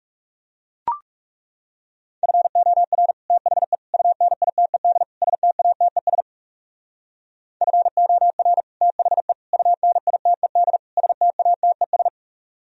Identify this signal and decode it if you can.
Morse code, heard as a keyed tone, about 38 wpm, 700 Hz. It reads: FOR THE UNITED STATES FOR THE UNITED STATES